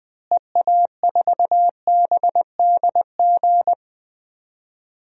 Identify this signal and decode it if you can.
Morse code, heard as a keyed tone, about 20 wpm, 700 Hz. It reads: EA4BDG
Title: DXCC-100 Call Signs - 6 Characters - Repeated 3x - 20wpm